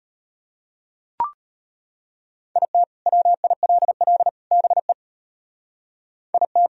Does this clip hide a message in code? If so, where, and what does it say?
Yes: Morse code from 2.6 to 6.7 s, heard as a keyed tone, about 31 wpm, 700 Hz. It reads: IT WILL BE IT